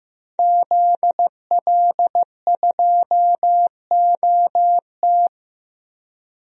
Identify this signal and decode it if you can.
Morse code, heard as a keyed tone, about 15 wpm, 700 Hz. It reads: ZL2OT
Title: DXCC-100 Call Signs - 5 Characters - 15wpm